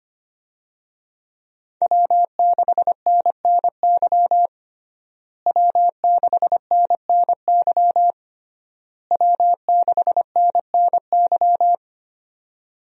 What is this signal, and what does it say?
Morse code, heard as a keyed tone, about 25 wpm, 700 Hz. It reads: W6NNY W6NNY W6NNY
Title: USA Call Signs - 5 Characters - Repeated 3x - 25wpm